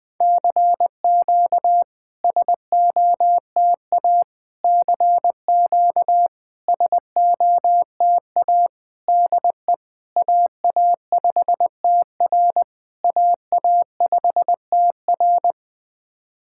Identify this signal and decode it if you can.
Morse code, heard as a keyed tone, about 21 wpm, 690 Hz. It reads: CQ SOTA CQ SOTA DE AA5TR AA5TR
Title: SOTA QSO Practice 20wpm